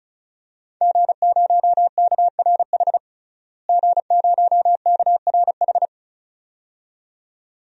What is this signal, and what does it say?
Morse code, heard as a keyed tone, about 34 wpm, 700 Hz. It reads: G0KRH G0KRH
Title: DXCC-100 Call Signs - 5 Characters - Repeated 2x - 35wpm